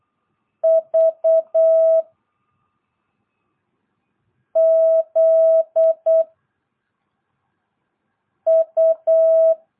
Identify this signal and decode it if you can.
Morse code, heard as a keyed tone, about 8 wpm, 660 Hz. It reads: V Z U
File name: jh8cbh / 2022cw